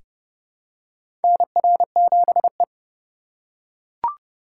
Morse code, heard as a keyed tone, about 30 wpm, 700 Hz. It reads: NR7E
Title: USA Call Signs - 4 Characters - 3x Character Spacing - 30wpm